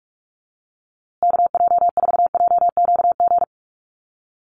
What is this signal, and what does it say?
Morse code, heard as a keyed tone, about 46 wpm, 700 Hz. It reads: KJ4JQG
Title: USA Call Signs Repeated 3x 45wpm